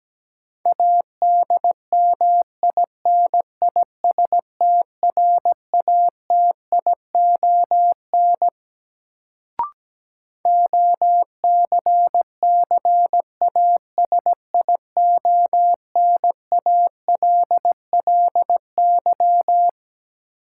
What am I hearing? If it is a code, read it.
Morse code, heard as a keyed tone, about 17 wpm, 700 Hz. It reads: ADMINISTRATION OCCASIONALLY
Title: Sentences from Top 1300 Words - New Word Drill Alt - 17wpm